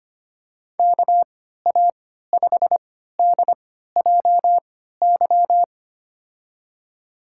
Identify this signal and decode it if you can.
Morse code, heard as a keyed tone, about 25 wpm, spaced as for 14 wpm, 700 Hz. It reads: KA5DJY
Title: USA Call Signs - 3x Character Spacing - 25wpm